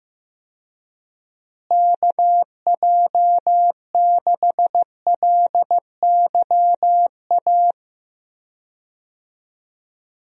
Morse code, heard as a keyed tone, about 15 wpm, 700 Hz. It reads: KJ6LYA
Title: USA Call Signs - 6 Characters - Repeated 2x - 15wpm